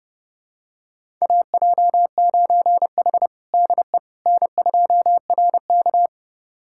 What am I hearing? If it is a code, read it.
Morse code, heard as a keyed tone, about 32 wpm, 700 Hz. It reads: AJ9H DE N2RK
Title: USA Callsign DE Callsign - 4 Characters - 30wpm